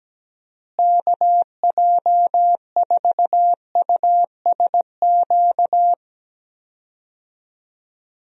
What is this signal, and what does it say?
Morse code, heard as a keyed tone, about 17 wpm, 700 Hz. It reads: KJ4USQ
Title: USA Call Signs - 6 Characters - Repeated 2x - 17wpm